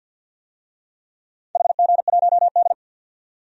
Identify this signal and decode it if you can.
Morse code, heard as a keyed tone, about 50 wpm, 700 Hz. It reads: HG1D
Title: DXCC-100 Call Signs - 4 Characters - 50wpm